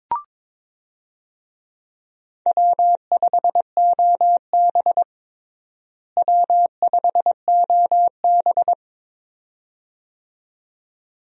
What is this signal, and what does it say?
Morse code, heard as a keyed tone, about 21 wpm, 700 Hz. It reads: W5OB W5OB
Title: USA Call Signs - 4 Characters - Repeated 2x - 22wpm